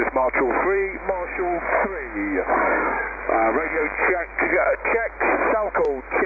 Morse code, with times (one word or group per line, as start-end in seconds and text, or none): none